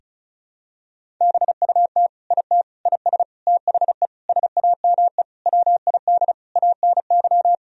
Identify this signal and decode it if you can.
Morse code, heard as a keyed tone, about 36 wpm, 700 Hz. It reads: BUT IT IS THE SAME WID ANY
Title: Sentences from Top 300 Words and Abbreviations 35wpm